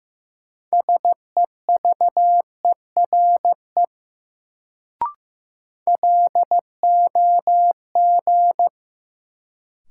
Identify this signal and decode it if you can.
Morse code, heard as a keyed tone, about 15 wpm, 700 Hz. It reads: SEVERE LOG
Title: Sentences from Top 1400 Words - New Word Drill Alt - 15wpm